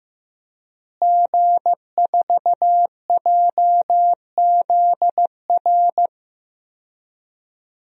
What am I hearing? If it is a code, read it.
Morse code, heard as a keyed tone, about 16 wpm, 700 Hz. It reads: G4JZR
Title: DXCC-100 Call Signs - 5 Characters - Repeated 3x - 15wpm